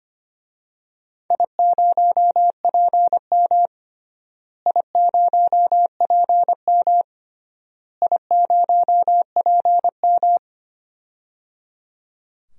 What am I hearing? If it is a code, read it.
Morse code, heard as a keyed tone, about 25 wpm, 700 Hz. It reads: I0PM I0PM I0PM
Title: DXCC-100 Call Signs - 4 Characters - Repeated 3x - 25wpm